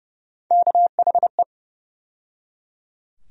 Morse code, heard as a keyed tone, about 30 wpm, 700 Hz. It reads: KHE
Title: USA Call Sign Postfixes - 3 Letters 30wpm